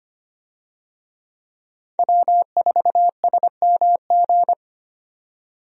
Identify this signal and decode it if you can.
Morse code, heard as a keyed tone, about 25 wpm, 700 Hz. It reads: W4SMG
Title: USA Call Signs - 5 Characters - 25wpm